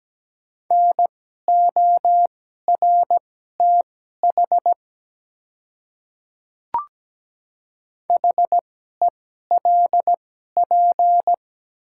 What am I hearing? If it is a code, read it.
Morse code, heard as a keyed tone, about 17 wpm, spaced as for 12 wpm, 700 Hz. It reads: NORTH HELP
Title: 2x Character Spacing - Top 300 Words - 17wpm